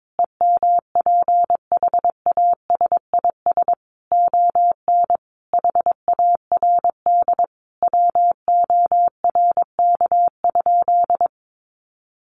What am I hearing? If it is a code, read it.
Morse code, heard as a keyed tone, about 22 wpm, 700 Hz. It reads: EMPHASIS ON HARD WORK?
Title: Sentences from Top 1400 Words 22wpm